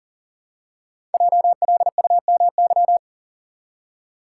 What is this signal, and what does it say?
Morse code, heard as a keyed tone, about 40 wpm, 700 Hz. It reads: JLUMY
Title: Code Groups - 5 Letters - 40wpm